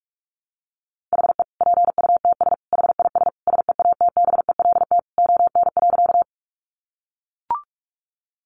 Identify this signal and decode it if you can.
Morse code, heard as a keyed tone, about 44 wpm, 700 Hz. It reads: HE PUTS HIS SEATBELT ON.
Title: Additional Easy Phrases 45wpm